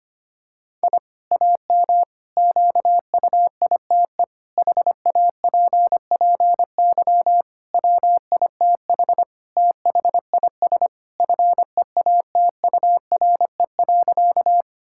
Morse code, heard as a keyed tone, about 25 wpm, 700 Hz. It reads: I AM QUITE HAPPY WITH THIS FEATURE.
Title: Sentences from Top 1000 Words - Review Part 5 - 25wpm